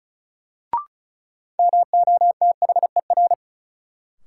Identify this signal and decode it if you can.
Morse code, heard as a keyed tone, about 36 wpm, 700 Hz. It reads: MOTHER
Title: Sentences from Top 200 Words - New Word Drill - 35wpm